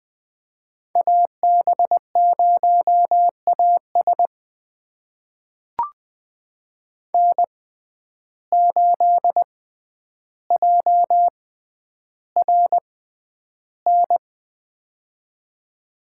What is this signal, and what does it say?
Morse code, heard as a keyed tone, about 21 wpm, spaced as for 10 wpm, 700 Hz. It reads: AB0AS N8JRN
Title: USA Call Signs - 5 Characters - 6x Character Spacing - 20wpm